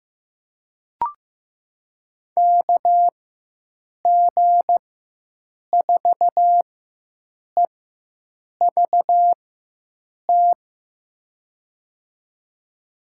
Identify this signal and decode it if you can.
Morse code, heard as a keyed tone, about 15 wpm, spaced as for 7 wpm, 700 Hz. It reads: KG4EVT
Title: USA Call Signs - 4x Character Spacing - 15wpm